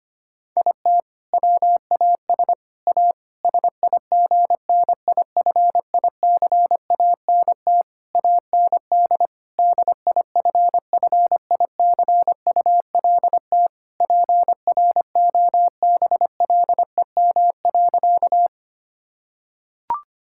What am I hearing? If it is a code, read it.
Morse code, heard as a keyed tone, about 25 wpm, 700 Hz. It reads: IT WAS A SIGNIFICANT AND DIFFICULT PROBLEM.
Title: Sentences from Top 1000 Words - Review Part 6 - 25wpm